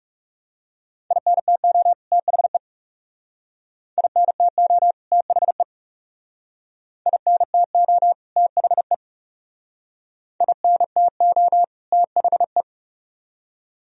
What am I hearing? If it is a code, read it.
Morse code, heard as a keyed tone, about 37 wpm, 700 Hz. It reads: INTO THE INTO THE INTO THE INTO THE
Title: Speed-Racing - Sets of 2 Words 30wpm